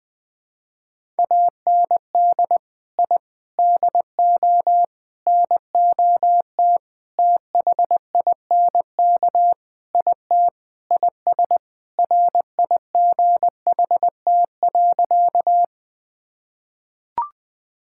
Morse code, heard as a keyed tone, about 20 wpm, 700 Hz. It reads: AND I DO NOT THINK IT IS RIGHT.
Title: Sentences with 5-Word Phrases - 20wpm